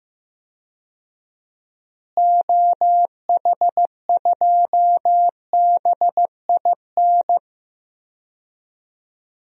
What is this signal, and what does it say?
Morse code, heard as a keyed tone, about 15 wpm, 700 Hz. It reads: OH2BIN
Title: DXCC-100 Call Signs - 6 Characters - Repeated 3x - 15wpm